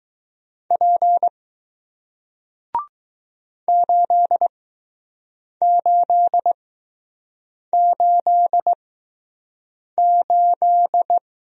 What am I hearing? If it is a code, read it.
Morse code, heard as a keyed tone, about 20 wpm, spaced as for 6 wpm, 700 Hz. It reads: P 8888